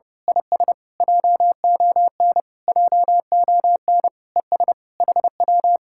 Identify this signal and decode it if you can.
Morse code, heard as a keyed tone, about 29 wpm, 700 Hz. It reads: IS JON JON ES HW
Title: Speed-Racing Common QSO Phrases - 30wpm